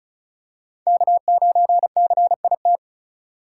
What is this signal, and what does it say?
Morse code, heard as a keyed tone, about 36 wpm, 700 Hz. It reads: K9CIT
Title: USA Call Signs - 5 Characters - 4x Character Spacing - 35wpm